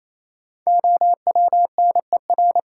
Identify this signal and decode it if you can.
Morse code, heard as a keyed tone, about 28 wpm, 700 Hz. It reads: OWNER
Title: Sentences from Top 1600 Words - New Word Drill - 28wpm